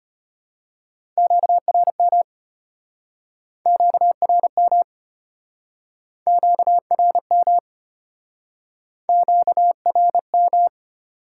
Morse code, heard as a keyed tone, about 34 wpm, 700 Hz. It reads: QRM QRM QRM QRM